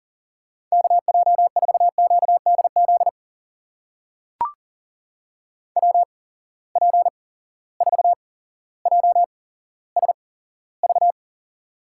Morse code, heard as a keyed tone, about 40 wpm, spaced as for 17 wpm, 700 Hz. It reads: KJ4QDZ WP4JSV
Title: USA Call Signs - 6 Characters - 8x Character Spacing - 40wpm